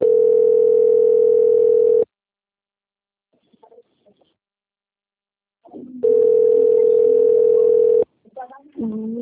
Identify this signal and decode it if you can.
Ringback tone (North America), 440 + 480 Hz, with speech.
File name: out- - - - - .45789